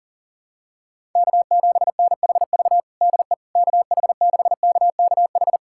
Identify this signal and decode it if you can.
Morse code, heard as a keyed tone, about 40 wpm, 700 Hz. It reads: K7NHV DE KH6KKH
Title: USA Callsign DE Callsign 40wpm